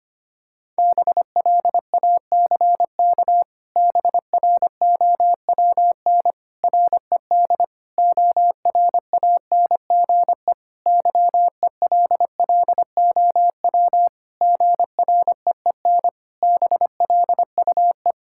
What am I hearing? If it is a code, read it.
Morse code, heard as a keyed tone, about 25 wpm, 700 Hz. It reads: BLACK BROWN RED ORANGE YELLOW GREEN BLUE
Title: Pangram Challenge v2 - 25wpm